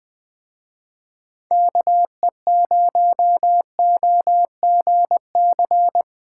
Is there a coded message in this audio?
Yes: Morse code from 1.5 to 6.0 s, heard as a keyed tone, about 20 wpm, 700 Hz. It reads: KE0OGC